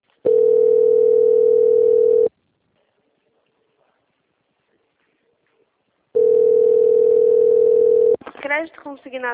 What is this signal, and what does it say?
Ringback tone (North America), 440 + 480 Hz, with speech.
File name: out- - - - - .44535